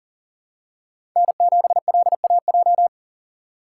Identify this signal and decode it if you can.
Morse code, heard as a keyed tone, about 39 wpm, 700 Hz. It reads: N7LAJ
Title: USA Call Signs - 5 Characters - 6x Character Spacing - 40wpm